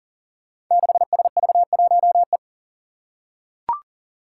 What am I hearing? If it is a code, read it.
Morse code, heard as a keyed tone, about 40 wpm, 700 Hz. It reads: 6SV1E
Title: Five Challenging Letter-Number Combinations 40wpm